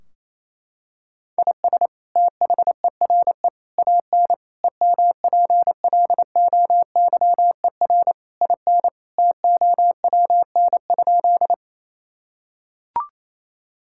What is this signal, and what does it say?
Morse code, heard as a keyed tone, about 28 wpm, 700 Hz. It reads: IS THERE AN EMPLOYER IN TOWN?
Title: Sentences from Top 1600 Words 28wpm